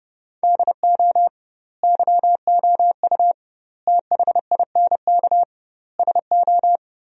Morse code, heard as a keyed tone, about 30 wpm, 700 Hz. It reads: DO YOU THINK SO
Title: 2x Word-Spacing - Sentences from Top 200 Words - 30wpm